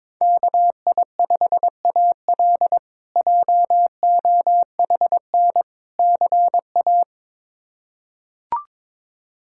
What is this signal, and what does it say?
Morse code, heard as a keyed tone, about 22 wpm, 700 Hz. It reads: KI5AL JOHN CA
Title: K1USN SST QSO Practice - 22wpm